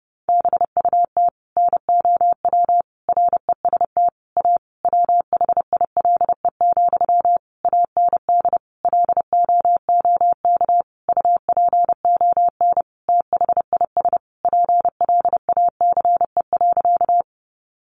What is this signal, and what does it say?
Morse code, heard as a keyed tone, about 30 wpm, 700 Hz. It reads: BUT NOW REST A WHILE, AND LOOK UPON THIS PLACE.
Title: Sentences from Top 1100 Words 30wpm